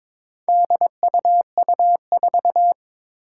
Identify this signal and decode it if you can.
Morse code, heard as a keyed tone, about 22 wpm, 700 Hz. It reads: DUU4
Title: Four Challenging Letter-Number Combinations 22wpm